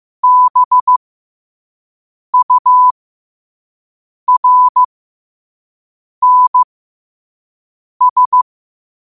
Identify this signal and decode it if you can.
Morse code, heard as a keyed tone, about 15 wpm, spaced as for 5 wpm, 1000 Hz. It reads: BURNS
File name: qotdfiles / 5wpm